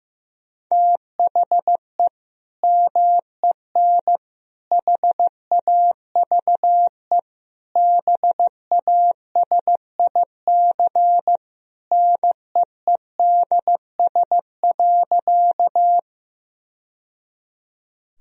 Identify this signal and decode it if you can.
Morse code, heard as a keyed tone, about 15 wpm, 700 Hz. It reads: THE MEN HAVE BASIC NEEDS.
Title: Sentences from Top 1000 Words - Review Part 3 - 15wpm